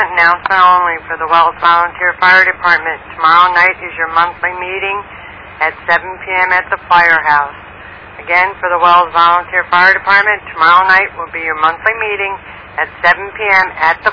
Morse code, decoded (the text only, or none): none